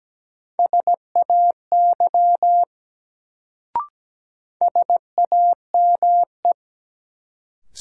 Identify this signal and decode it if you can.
Morse code, heard as a keyed tone, about 17 wpm, 700 Hz. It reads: SAY SAME
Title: Sentences from Top 200 Words - New Word Drill - 17wpm